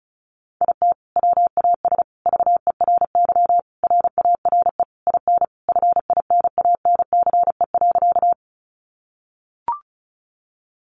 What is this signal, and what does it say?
Morse code, heard as a keyed tone, about 34 wpm, 700 Hz. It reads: IT WAS VERY RARE IN FINANCE.
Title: Sentences from Top 1500 Words 35wpm